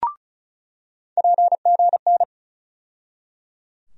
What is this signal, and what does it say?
Morse code, heard as a keyed tone, about 34 wpm, 700 Hz. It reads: PGN
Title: USA Call Sign Postfixes - 3 Letters 35wpm